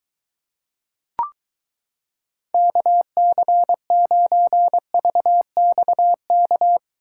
Morse code, heard as a keyed tone, about 22 wpm, 700 Hz. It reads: KC9VXK